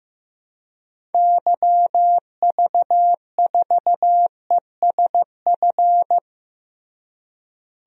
Morse code, heard as a keyed tone, about 15 wpm, 700 Hz. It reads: YV4ESF
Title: DXCC-100 Call Signs - 6 Characters - Repeated 3x - 15wpm